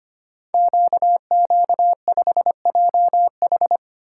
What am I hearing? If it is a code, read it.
Morse code, heard as a keyed tone, about 25 wpm, 700 Hz. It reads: QQ5JH
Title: Code Groups - 5 Letter:Numbers - 25wpm